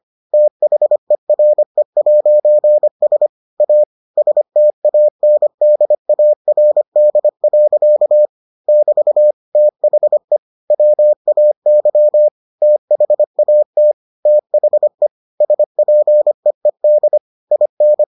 Morse code, heard as a keyed tone, about 25 wpm, 600 Hz. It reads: THERE'S A STANDARD. <BT> THE WAY THAT THE SPEED IN